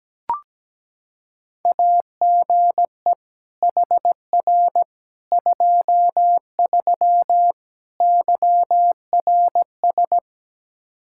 Morse code, heard as a keyed tone, about 17 wpm, 700 Hz. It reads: AGE HR 23 YRS
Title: Practice QSO - Exchange 3 - More Info - 17wpm